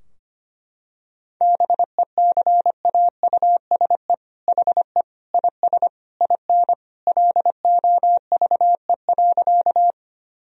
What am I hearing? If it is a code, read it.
Morse code, heard as a keyed tone, about 25 wpm, 700 Hz. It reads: BECAUSE HE IS IN LOVE.
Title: Sentences from Top 1000 Words - Review Part 3 - 25wpm